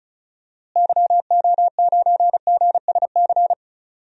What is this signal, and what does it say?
Morse code, heard as a keyed tone, about 34 wpm, 700 Hz. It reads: YO9GSC
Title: DXCC-100 Call Signs - 6 Characters - Repeated 3x - 35wpm